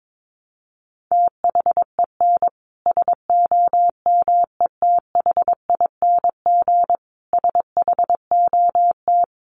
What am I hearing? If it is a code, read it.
Morse code, heard as a keyed tone, about 22 wpm, 700 Hz. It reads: THEN SOMETHING SHOT
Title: Sentences from Top 1100 Words 22wpm